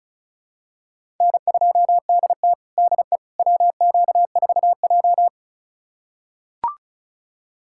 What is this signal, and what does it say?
Morse code, heard as a keyed tone, about 36 wpm, 690 Hz. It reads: N2DT DE WQ4J
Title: USA Callsign DE Callsign - 4 Characters - 35wpm